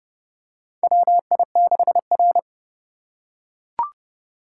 Morse code, heard as a keyed tone, about 30 wpm, 700 Hz. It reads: WI6R
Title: USA Call Signs - 4 Characters - 2x Character Spacing - 30wpm